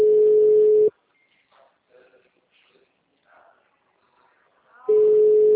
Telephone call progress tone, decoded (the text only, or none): ringback tone